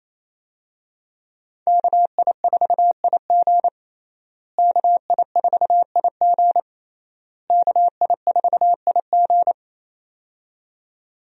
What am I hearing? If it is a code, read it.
Morse code, heard as a keyed tone, about 28 wpm, 700 Hz. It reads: KI4IG KI4IG KI4IG
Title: USA Call Signs - 5 Characters - Repeated 3x - 28wpm